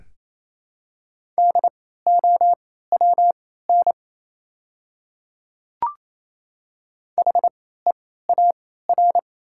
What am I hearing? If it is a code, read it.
Morse code, heard as a keyed tone, about 28 wpm, spaced as for 16 wpm, 700 Hz. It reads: DOWN HEAR
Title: 3x Character Spacing - Top 500 Words - 28wpm